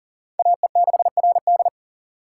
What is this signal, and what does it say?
Morse code, heard as a keyed tone, about 40 wpm, 700 Hz. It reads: AE6RD
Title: USA Call Signs - 5 Characters - 3x Character Spacing - 40wpm